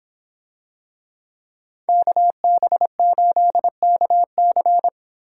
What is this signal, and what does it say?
Morse code, heard as a keyed tone, about 26 wpm, 700 Hz. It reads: KB8KC